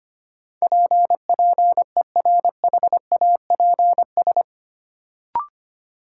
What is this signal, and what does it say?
Morse code, heard as a keyed tone, about 25 wpm, 700 Hz. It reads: PPERHAPS
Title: Sentences from Top 1100 Words - New Word Drill - 25wpm